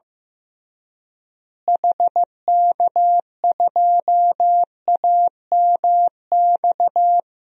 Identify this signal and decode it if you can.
Morse code, heard as a keyed tone, about 15 wpm, 700 Hz. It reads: HK2AMX
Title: DXCC-100 Call Signs - 6 Characters - Repeated 2x - 15wpm